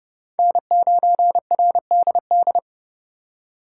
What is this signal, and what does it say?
Morse code, heard as a keyed tone, about 30 wpm, 700 Hz. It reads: N9RDD
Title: USA Call Signs - 5 Characters - 3x Character Spacing - 30wpm